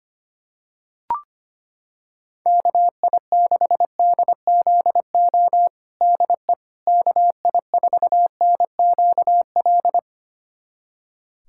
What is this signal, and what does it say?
Morse code, heard as a keyed tone, about 25 wpm, 700 Hz. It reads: KI6DZO DE KI4NQL
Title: USA Callsign DE Callsign 25wpm